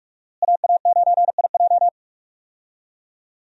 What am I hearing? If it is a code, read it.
Morse code, heard as a keyed tone, about 46 wpm, 700 Hz. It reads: AA9IJ
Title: USA Call Signs - 5 Characters - Repeated 2x - 45wpm